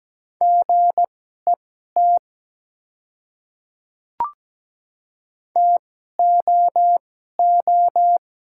Morse code, heard as a keyed tone, about 17 wpm, spaced as for 12 wpm, 700 Hz. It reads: GET TOO
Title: 2x Character Spacing - Top 300 Words - 17wpm